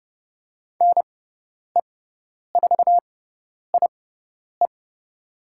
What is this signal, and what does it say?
Morse code, heard as a keyed tone, about 30 wpm, spaced as for 10 wpm, 700 Hz. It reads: NE4IE
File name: Lesson 18g - Letter m - Call Signs 30wpm